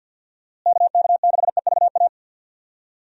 Morse code, heard as a keyed tone, about 50 wpm, 700 Hz. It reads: KK6EVA
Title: USA Call Signs - 6 Characters - Repeated 2x - 50wpm